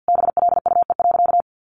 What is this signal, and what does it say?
Morse code, heard as a keyed tone, about 50 wpm, 700 Hz. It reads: BLUE.
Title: Additional Easy Phrases 50wpm